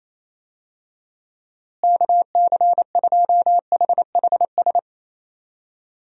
Morse code, heard as a keyed tone, about 28 wpm, 700 Hz. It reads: KC2HHS
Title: USA Call Signs - 6 Characters - Repeated 2x - 28wpm